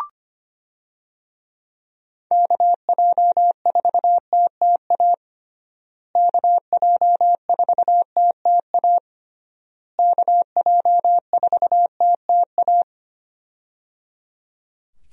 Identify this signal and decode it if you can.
Morse code, heard as a keyed tone, about 25 wpm, 700 Hz. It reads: KJ4TTA KJ4TTA KJ4TTA
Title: USA Call Signs - 6 Characters - Repeated 3x - 25wpm